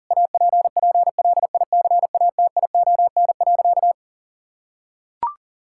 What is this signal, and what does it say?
Morse code, heard as a keyed tone, about 40 wpm, 700 Hz. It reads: APPLICATION.
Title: Sentences from Top 1600 Words 40wpm